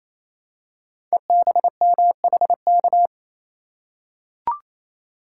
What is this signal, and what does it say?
Morse code, heard as a keyed tone, about 27 wpm, 700 Hz. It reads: EBMHK
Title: Code Groups - 5 Letters - 28wpm